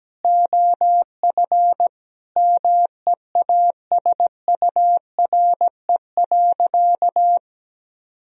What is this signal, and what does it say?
Morse code, heard as a keyed tone, about 17 wpm, 700 Hz. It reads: OF MEASURE.
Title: Sentences from Top 1000 Words - Review Part 3 - 17wpm